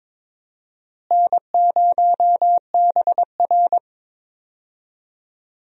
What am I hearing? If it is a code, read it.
Morse code, heard as a keyed tone, about 22 wpm, 700 Hz. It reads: N0BR
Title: USA Call Signs - 4 Characters - Repeated 3x - 22wpm